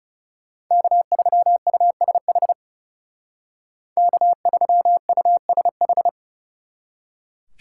Morse code, heard as a keyed tone, about 40 wpm, 700 Hz. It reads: K3USH K3USH